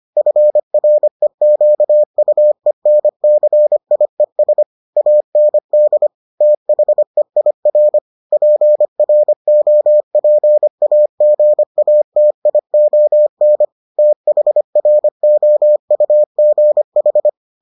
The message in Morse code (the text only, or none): FREQUENCIES AND THEIR PROPAGATION THROUGH